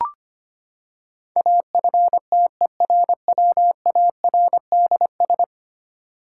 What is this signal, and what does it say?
Morse code, heard as a keyed tone, about 25 wpm, 700 Hz. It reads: AFTERWARDS
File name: Ten Letter Words - 25wpm